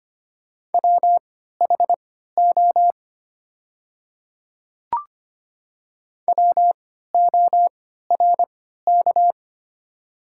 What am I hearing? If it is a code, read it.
Morse code, heard as a keyed tone, about 25 wpm, spaced as for 14 wpm, 690 Hz. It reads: WHO WORK
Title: 3x Character Spacing - Top 500 Words - 25wpm